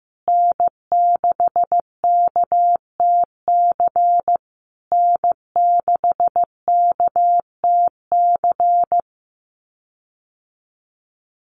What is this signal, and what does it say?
Morse code, heard as a keyed tone, about 15 wpm, 700 Hz. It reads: N6KTC N6KTC
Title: USA Call Signs Repeated 3x 15wpm